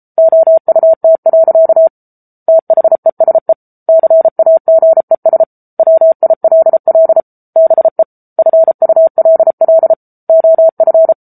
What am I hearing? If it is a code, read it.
Morse code, heard as a keyed tone, about 33 wpm, 650 Hz. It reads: OUT. THESE CAGES WILL BE FULL OF
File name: Book Two / 07-Chapter 7 - 33 WPM